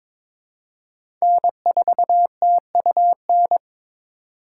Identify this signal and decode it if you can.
Morse code, heard as a keyed tone, about 23 wpm, 700 Hz. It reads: N4TUN
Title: USA Call Signs - 5 Characters - 2x Character Spacing - 22wpm